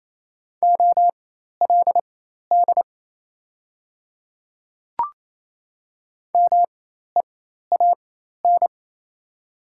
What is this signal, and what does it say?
Morse code, heard as a keyed tone, about 28 wpm, spaced as for 13 wpm, 690 Hz. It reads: OLD MEAN
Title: 4x Character Spacing - Top 300 Words - 28wpm